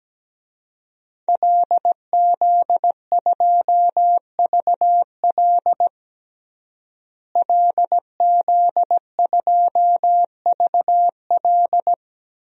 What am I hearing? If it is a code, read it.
Morse code, heard as a keyed tone, about 17 wpm, 700 Hz. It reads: LZ2VL LZ2VL
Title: DXCC-100 Call Signs - 5 Characters - Repeated 3x - 17wpm